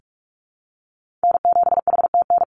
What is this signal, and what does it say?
Morse code, heard as a keyed tone, about 46 wpm, 700 Hz. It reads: N7HTN
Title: USA Call Signs 45wpm